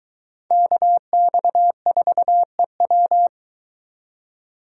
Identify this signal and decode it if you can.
Morse code, heard as a keyed tone, about 23 wpm, 700 Hz. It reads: KX4EW